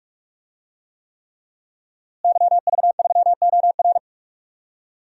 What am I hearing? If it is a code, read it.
Morse code, heard as a keyed tone, about 42 wpm, 700 Hz. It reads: YV3OR
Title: DXCC-100 Call Signs - 5 Characters - Repeated 3x - 45wpm